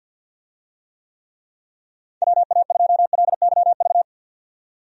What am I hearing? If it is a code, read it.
Morse code, heard as a keyed tone, about 50 wpm, 700 Hz. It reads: WA2LYV
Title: USA Call Signs - 6 Characters - 50wpm